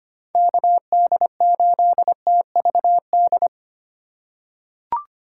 Morse code, heard as a keyed tone, about 25 wpm, 700 Hz. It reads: KD8TVD
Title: USA Call Signs - 6 Characters - 2x Character Spacing - 25wpm